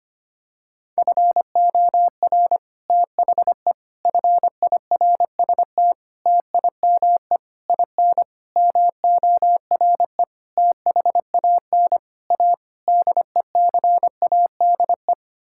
Morse code, heard as a keyed tone, about 25 wpm, 700 Hz. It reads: FOR THE FIRST TIME IN MORE THAN A DECADE
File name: Sets of 9 Words - 25wpm